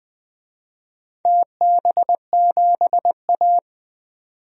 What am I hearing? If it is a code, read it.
Morse code, heard as a keyed tone, about 20 wpm, 700 Hz. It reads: TB7A
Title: DXCC-100 Call Signs - 4 Characters - 20wpm